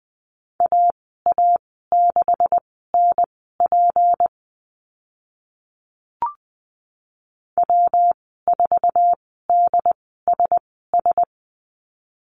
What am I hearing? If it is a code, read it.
Morse code, heard as a keyed tone, about 20 wpm, spaced as for 14 wpm, 700 Hz. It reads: AA6NP W4DSS
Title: 154 Lesson 25g - Letter p - Call Signs 20wpm